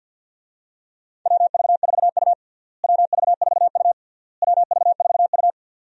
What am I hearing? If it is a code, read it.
Morse code, heard as a keyed tone, about 52 wpm, 700 Hz. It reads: WV4U WV4U WV4U
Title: USA Call Signs - 4 Characters - Repeated 3x - 50wpm